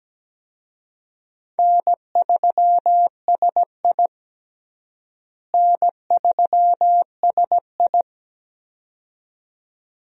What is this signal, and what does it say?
Morse code, heard as a keyed tone, about 17 wpm, 700 Hz. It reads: N3SI N3SI
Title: USA Call Signs - 4 Characters - Repeated 2x - 17wpm